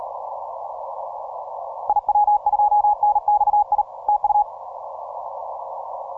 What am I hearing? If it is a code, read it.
Morse code, heard as a keyed tone, about 37 wpm, 840 Hz. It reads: IW2NXI TU
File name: simple-file-list / TO6OK 10MCW